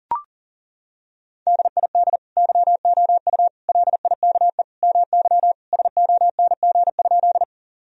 Morse code, heard as a keyed tone, about 40 wpm, 700 Hz. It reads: DID YOU LIKE MY SONG?
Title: Sentences from Top 1000 Words - Review Part 2 - 40wpm